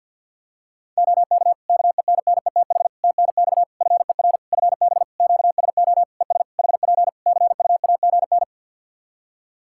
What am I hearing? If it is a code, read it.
Morse code, heard as a keyed tone, about 50 wpm, 700 Hz. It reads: OK KENNETH TNX FER FB QSO ES HP CUAGN
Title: Practice QSO - Exchange 4 - Ending - 50wpm